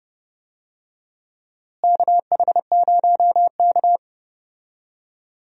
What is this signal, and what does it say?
Morse code, heard as a keyed tone, about 30 wpm, 700 Hz. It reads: KH0K
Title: USA Call Signs - 4 Characters - Repeated 2x - 30wpm